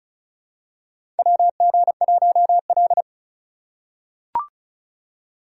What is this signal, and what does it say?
Morse code, heard as a keyed tone, about 36 wpm, 700 Hz. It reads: WG1L
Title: USA Call Signs - 4 Characters - 6x Character Spacing - 35wpm